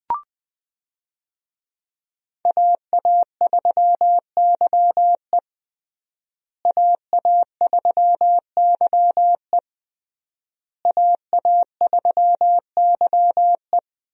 Morse code, heard as a keyed tone, about 20 wpm, 700 Hz. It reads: AA3YE AA3YE AA3YE
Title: USA Call Signs - 5 Characters - Repeated 3x - 20wpm